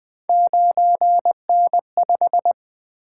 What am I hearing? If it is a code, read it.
Morse code, heard as a keyed tone, about 20 wpm, 700 Hz. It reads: 9N5
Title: Three Letter-Number Combinations - 20wpm